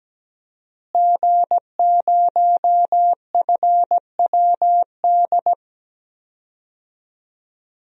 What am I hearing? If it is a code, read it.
Morse code, heard as a keyed tone, about 17 wpm, 700 Hz. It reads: G0FWD
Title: DXCC-100 Call Signs - 5 Characters - Repeated 2x - 17wpm